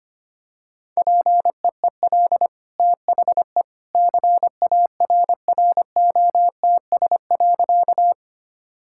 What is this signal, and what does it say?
Morse code, heard as a keyed tone, about 25 wpm, 700 Hz. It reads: PEEL THE CARROTS.
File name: Easy Sentences with Hints - 25wpm